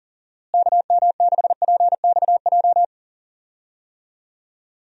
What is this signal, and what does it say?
Morse code, heard as a keyed tone, about 40 wpm, 700 Hz. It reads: KM6PXJ
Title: USA Call Signs - 6 Characters - Repeated 3x - 40wpm